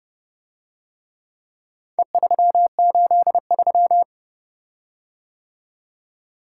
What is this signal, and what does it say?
Morse code, heard as a keyed tone, about 29 wpm, 700 Hz. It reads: E383